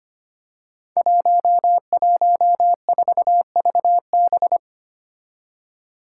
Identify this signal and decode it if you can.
Morse code, heard as a keyed tone, about 25 wpm, 700 Hz. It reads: 114VB
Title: Five Challenging Letter-Number Combinations 25wpm